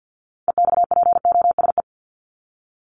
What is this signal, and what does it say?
Morse code, heard as a keyed tone, about 48 wpm, 700 Hz. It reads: EXPOSE
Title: Top 1501-1750 Words 50wpm